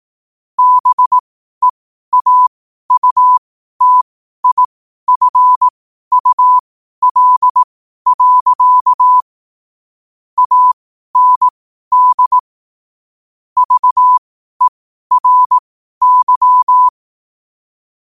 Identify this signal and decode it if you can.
Morse code, heard as a keyed tone, about 18 wpm, spaced as for 12 wpm, 1000 Hz. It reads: BEAUTIFUL. AND VERY
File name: qotdfiles / 12wpm